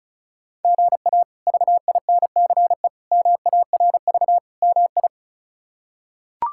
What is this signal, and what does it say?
Morse code, heard as a keyed tone, about 34 wpm, 700 Hz. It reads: GA VINCE MARV MI
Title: K1USN SST QSO Practice - 35wpm